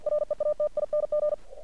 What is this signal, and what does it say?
Morse code, heard as a keyed tone, about 49 wpm, 600 Hz. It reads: REATING
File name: archived / CRSnet